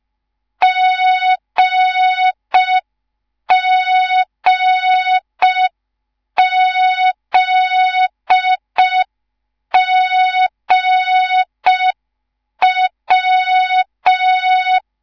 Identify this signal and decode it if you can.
Morse code, heard as a keyed tone, about 5 wpm, 3700 Hz. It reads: GGZGW
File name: Morze-HA2MN / hang / 10GWZ HA2MN